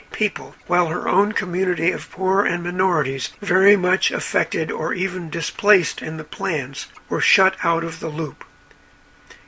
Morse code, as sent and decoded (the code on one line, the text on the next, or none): none